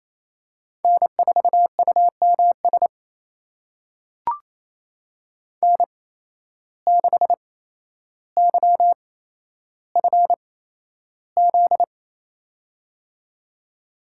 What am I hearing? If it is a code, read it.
Morse code, heard as a keyed tone, about 28 wpm, spaced as for 12 wpm, 700 Hz. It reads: N4UMS N6YFZ
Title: USA Call Signs - 8x Character Spacing - 28wpm